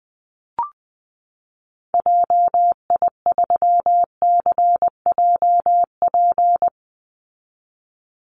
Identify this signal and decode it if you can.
Morse code, heard as a keyed tone, about 20 wpm, 700 Hz. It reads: JI3CJP
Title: Call Signs Repeated 2x - 20wpm